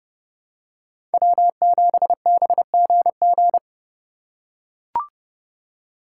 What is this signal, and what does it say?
Morse code, heard as a keyed tone, about 31 wpm, 700 Hz. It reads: W7BGG
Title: USA Call Signs - 5 Characters - 4x Character Spacing - 30wpm